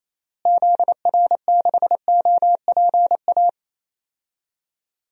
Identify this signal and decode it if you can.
Morse code, heard as a keyed tone, about 28 wpm, 700 Hz. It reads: ZR6OPA
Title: DXCC-100 Call Signs - 6 Characters - Repeated 2x - 28wpm